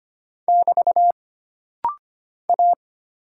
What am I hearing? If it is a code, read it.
Morse code, heard as a keyed tone, about 25 wpm, 700 Hz. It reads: <BT> A
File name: Single Character (Letter:Number:Punctuation:Prosign) - Rapid-Fire - 25wpm